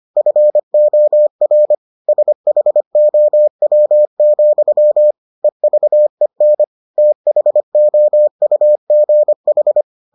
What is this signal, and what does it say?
Morse code, heard as a keyed tone, about 25 wpm, 600 Hz. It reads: FOR SHOW, EVEN THOUGH